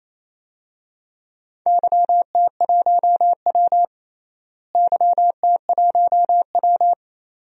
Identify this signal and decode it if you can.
Morse code, heard as a keyed tone, about 28 wpm, 700 Hz. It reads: YT1W YT1W
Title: DXCC-100 Call Signs - 4 Characters - Repeated 2x - 28wpm